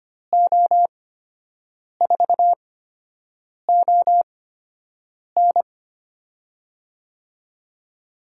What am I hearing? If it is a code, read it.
Morse code, heard as a keyed tone, about 25 wpm, spaced as for 7 wpm, 700 Hz. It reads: O4ON
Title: USA Call Signs - 8x Character Spacing - 25wpm